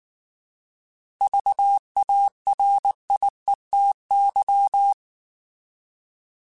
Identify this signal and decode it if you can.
Morse code, heard as a keyed tone, about 19 wpm, 800 Hz. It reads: VARIETY